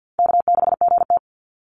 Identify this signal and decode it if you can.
Morse code, heard as a keyed tone, about 50 wpm, 700 Hz. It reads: K6GT
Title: Call Signs Repeated 3x - 50wpm